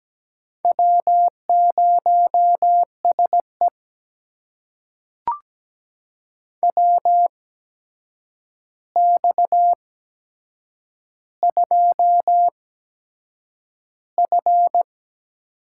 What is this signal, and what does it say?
Morse code, heard as a keyed tone, about 17 wpm, spaced as for 7 wpm, 700 Hz. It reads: W0SE WX2F
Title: USA Call Signs - 4 Characters - 8x Character Spacing - 17wpm